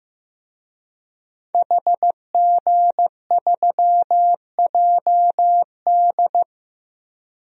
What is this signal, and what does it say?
Morse code, heard as a keyed tone, about 15 wpm, 700 Hz. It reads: HG3JD
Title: DXCC-100 Call Signs - 5 Characters - Repeated 3x - 15wpm